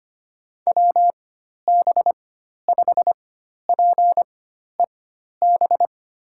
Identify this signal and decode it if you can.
Morse code, heard as a keyed tone, about 25 wpm, spaced as for 12 wpm, 700 Hz. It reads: WB5PEB
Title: USA Call Signs - 6 Characters - 4x Character Spacing - 25wpm